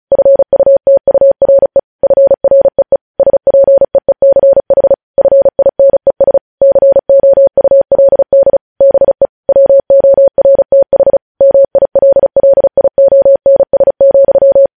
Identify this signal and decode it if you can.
Morse code, heard as a keyed tone, about 36 wpm, 550 Hz. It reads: FUTURE FREE SPEECH FINES COULD BE WORTH MILLIONS,
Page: News Headlines in Morse Code at 35 WPM